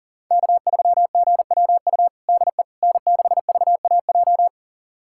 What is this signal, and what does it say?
Morse code, heard as a keyed tone, about 40 wpm, 700 Hz. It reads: K3GWU DE N6VAJ
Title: USA Callsign DE Callsign - 5 Characters - 40wpm